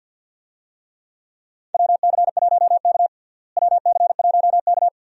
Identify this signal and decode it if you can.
Morse code, heard as a keyed tone, about 49 wpm, 700 Hz. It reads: WC1K WC1K
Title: USA Call Signs - 4 Characters - Repeated 2x - 50wpm